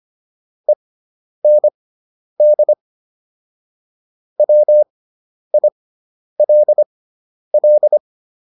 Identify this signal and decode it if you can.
Morse code, heard as a keyed tone, about 25 wpm, spaced as for 10 wpm, 600 Hz. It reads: END WILL